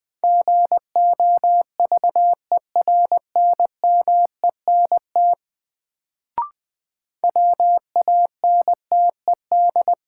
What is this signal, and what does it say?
Morse code, heard as a keyed tone, about 20 wpm, 700 Hz. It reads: GOVERNMENT WANTED
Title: Sets of 2 Words - New Word Drill - 20wpm